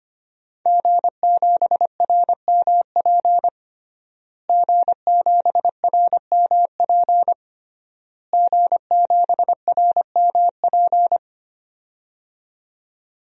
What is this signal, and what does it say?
Morse code, heard as a keyed tone, about 25 wpm, 700 Hz. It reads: G7RMP G7RMP G7RMP
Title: DXCC-100 Call Signs - 5 Characters - Repeated 3x - 25wpm